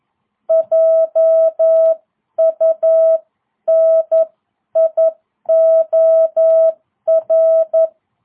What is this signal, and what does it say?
Morse code, heard as a keyed tone, about 11 wpm, 660 Hz. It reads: JUNIOR